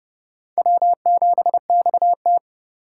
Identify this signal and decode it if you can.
Morse code, heard as a keyed tone, about 32 wpm, 700 Hz. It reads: W7XT
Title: K1USN SST QSO Practice - 30wpm